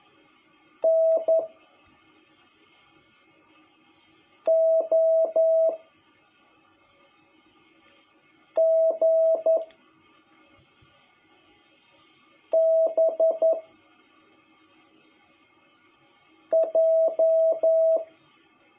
Morse code, heard as a keyed tone, about 11 wpm, 660 Hz. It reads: N O G B J
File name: jh8cbh / 2021cw